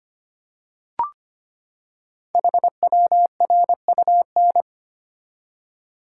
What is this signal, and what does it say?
Morse code, heard as a keyed tone, about 25 wpm, 700 Hz. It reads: HWRUN
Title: Code Groups - 5 Letters - 25wpm